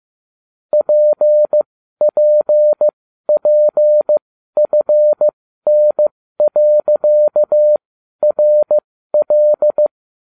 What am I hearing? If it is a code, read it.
Morse code, heard as a keyed tone, about 15 wpm, spaced as for 12 wpm, 600 Hz. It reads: PPPFN.RL